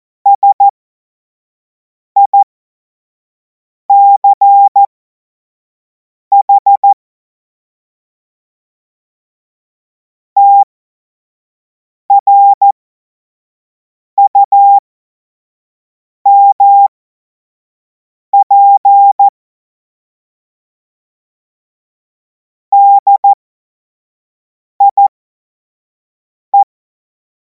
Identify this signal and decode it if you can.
Morse code, heard as a keyed tone, about 14 wpm, spaced as for 5 wpm, 800 Hz. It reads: SICH TRUMP DIE
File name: Morsekurs / Witze / Witz 05. Mai 2026 25 F70